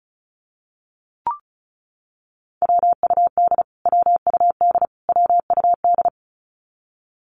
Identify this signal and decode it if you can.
Morse code, heard as a keyed tone, about 34 wpm, 700 Hz. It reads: WUD WUD WUD
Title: Abbreviations Repeated 3x - 35wpm